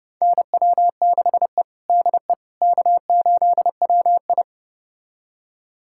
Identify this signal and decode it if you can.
Morse code, heard as a keyed tone, about 30 wpm, 700 Hz. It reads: NW6E DE K8WI
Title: USA Callsign DE Callsign - 4 Characters - 30wpm